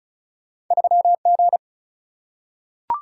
Morse code, heard as a keyed tone, about 36 wpm, 700 Hz. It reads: 3G
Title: Single-Letter Single-Number 35wpm